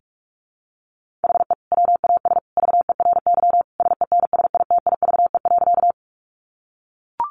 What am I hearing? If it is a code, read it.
Morse code, heard as a keyed tone, about 44 wpm, 700 Hz. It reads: HE WAS VERY SENSITIVE.
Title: Sentences from Top 1400 Words 45wpm